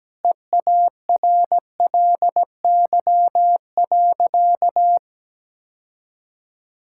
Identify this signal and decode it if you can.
Morse code, heard as a keyed tone, about 17 wpm, 700 Hz. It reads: EARLY.
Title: Sentences from Top 1000 Words - Review Part 6 - 17wpm